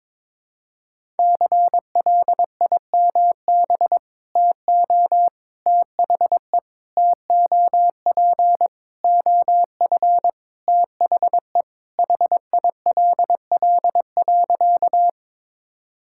Morse code, heard as a keyed tone, about 22 wpm, 700 Hz. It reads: CLIMB TO THE TOP OF THE HILL.
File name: practice / 22wpm / Sentences with 4-Word Phrases - 22wpm